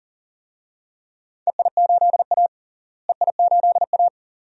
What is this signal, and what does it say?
Morse code, heard as a keyed tone, about 40 wpm, 700 Hz. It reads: EI8A EI8A
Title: DXCC-100 Call Signs - 4 Characters - Repeated 2x - 40wpm